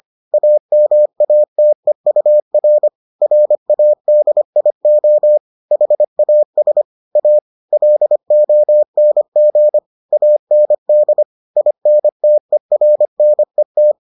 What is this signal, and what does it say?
Morse code, heard as a keyed tone, about 25 wpm, 600 Hz. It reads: AMATEUR RADIO HAS A LONG AND INTERNET